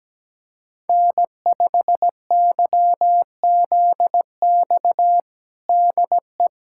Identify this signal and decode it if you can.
Morse code, heard as a keyed tone, about 17 wpm, 700 Hz. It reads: N5YZX DE
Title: USA Callsign DE Callsign - 5 Characters - 17wpm